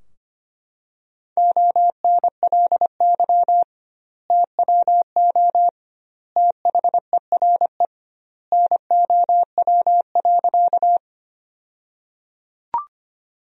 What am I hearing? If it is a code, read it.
Morse code, heard as a keyed tone, about 25 wpm, 700 Hz. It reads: ONLY TWO THERE NOW.
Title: 2x Word-Spacing - Sentences from Top 200 Words - 25wpm